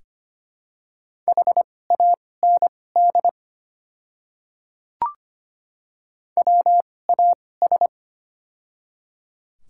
Morse code, heard as a keyed tone, about 25 wpm, spaced as for 18 wpm, 700 Hz. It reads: HAND WAS
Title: 2x Character Spacing - Top 500 Words - 25wpm